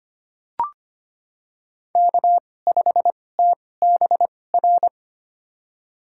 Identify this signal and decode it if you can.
Morse code, heard as a keyed tone, about 25 wpm, spaced as for 18 wpm, 700 Hz. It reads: K5TBR
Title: USA Call Signs - 5 Characters - 2x Character Spacing - 25wpm